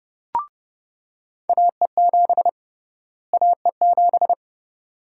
Morse code, heard as a keyed tone, about 31 wpm, 700 Hz. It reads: AE7 AE7
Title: USA Call Sign Prefixes - Repeated 2x - 30wpm